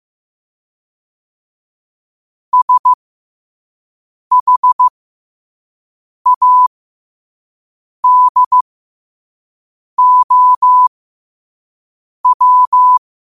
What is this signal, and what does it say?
Morse code, heard as a keyed tone, about 15 wpm, spaced as for 5 wpm, 1000 Hz. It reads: SHADOW
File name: qotdfiles / 5wpm